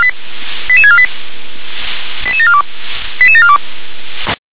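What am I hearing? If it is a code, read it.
Five-tone selective call (ZVEI), heard as quick runs of five tones, tones of about 70 ms. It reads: 88648 88742 88742
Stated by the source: ZVEI five-tone selective call received by SDR.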